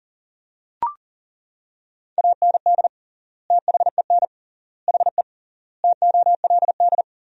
Morse code, heard as a keyed tone, about 40 wpm, 700 Hz. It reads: AND THEN HE TOLD
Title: 3x Word-Spacing - Sentences from Top 400 Words - 40wpm